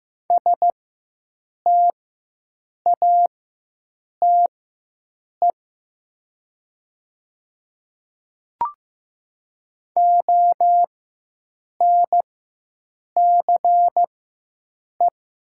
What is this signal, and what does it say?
Morse code, heard as a keyed tone, about 15 wpm, spaced as for 7 wpm, 700 Hz. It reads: STATE ONCE
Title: 4x Character Spacing - Top 500 Words - 15wpm